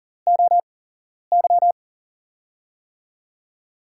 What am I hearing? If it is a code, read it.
Morse code, heard as a keyed tone, about 38 wpm, spaced as for 11 wpm, 700 Hz. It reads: OY
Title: USA Call Signs - 5 Characters - 8x Character Spacing - 40wpm